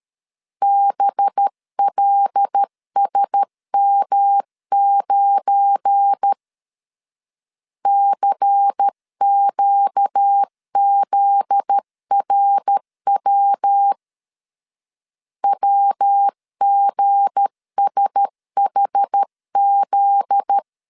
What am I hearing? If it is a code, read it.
Morse code, heard as a keyed tone, about 13 wpm, 800 Hz. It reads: BLSM9 CQZRW WGSHZ
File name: LEZIONI / 7 GF9Z / 7 GF9Z 12wpm 2147